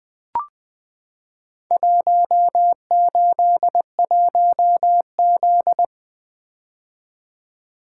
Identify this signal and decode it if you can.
Morse code, heard as a keyed tone, about 20 wpm, 700 Hz. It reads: 181Z
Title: Four Letter-Number Combinations - 20wpm